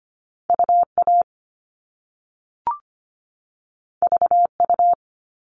Morse code, heard as a keyed tone, about 25 wpm, 700 Hz. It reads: UA VU
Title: All Cut Numbers - 2-Digit 25wpm